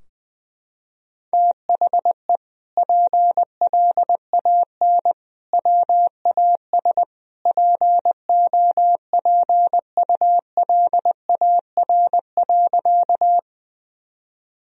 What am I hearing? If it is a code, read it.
Morse code, heard as a keyed tone, about 20 wpm, 700 Hz. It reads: THE PLAN WAS POPULAR.
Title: Sentences from Top 1000 Words - Review Part 3 - 20wpm